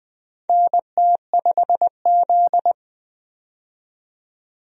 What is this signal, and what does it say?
Morse code, heard as a keyed tone, about 21 wpm, 700 Hz. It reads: NT5Z
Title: USA Call Signs - 4 Characters - 20wpm